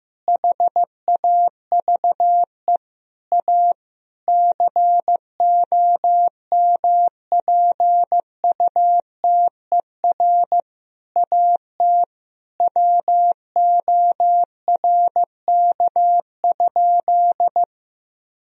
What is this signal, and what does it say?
Morse code, heard as a keyed tone, about 15 wpm, 700 Hz. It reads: HAVE A COMPUTER AT WORK?
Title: Sentences from Top 1000 Words - Review Part 3 - 15wpm